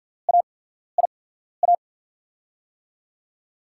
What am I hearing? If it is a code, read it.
Morse code, heard as a keyed tone, about 53 wpm, spaced as for 14 wpm, 690 Hz. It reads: AIA
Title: USA Call Signs - 6 Characters - 8x Character Spacing - 50wpm